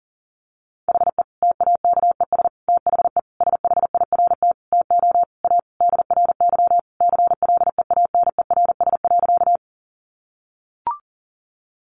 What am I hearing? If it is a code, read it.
Morse code, heard as a keyed tone, about 40 wpm, 700 Hz. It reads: HE TAKES THE SHIRT TO A DRY CLEANERS.
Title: Additional Easy Phrases 40wpm